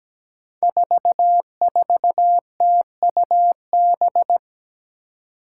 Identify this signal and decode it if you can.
Morse code, heard as a keyed tone, about 17 wpm, 700 Hz. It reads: 44TUB
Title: Five Challenging Letter-Number Combinations 17wpm